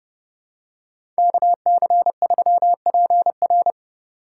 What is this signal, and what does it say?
Morse code, heard as a keyed tone, about 29 wpm, 700 Hz. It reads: KC3PR
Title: USA Call Signs - 5 Characters - 3x Character Spacing - 30wpm